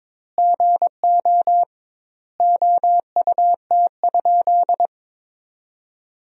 Morse code, heard as a keyed tone, about 22 wpm, 700 Hz. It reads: GO OUT?
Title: 2x Word-Spacing - Sentences from Top 300 Words - 22wpm